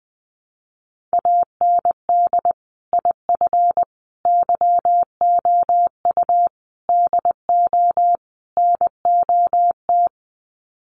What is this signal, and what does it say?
Morse code, heard as a keyed tone, about 20 wpm, 700 Hz. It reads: AND IF YOU DO NOT
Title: Sets of 5 Words 20wpm